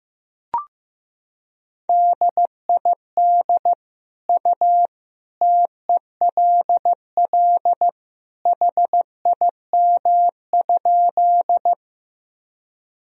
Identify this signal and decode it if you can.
Morse code, heard as a keyed tone, about 15 wpm, 700 Hz. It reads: DID U TELL HIM?
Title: Sentences from Top 200 Words and Abbreviations 15wpm